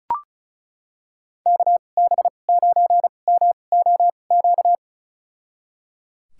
Morse code, heard as a keyed tone, about 36 wpm, spaced as for 25 wpm, 700 Hz. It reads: KB9MOQ
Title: USA Call Signs - 6 Characters - 2x Character Spacing - 35wpm